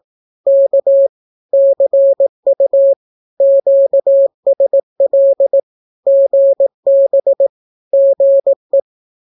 Morse code, heard as a keyed tone, about 18 wpm, 550 Hz. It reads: K CU QSL GB GE
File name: naukapodstaw / skroty / skroty